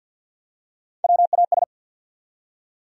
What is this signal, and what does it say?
Morse code, heard as a keyed tone, about 47 wpm, 700 Hz. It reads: WAS
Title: Top 100 Words 50wpm